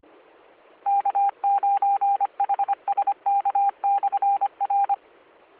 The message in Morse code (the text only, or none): K9HSK/R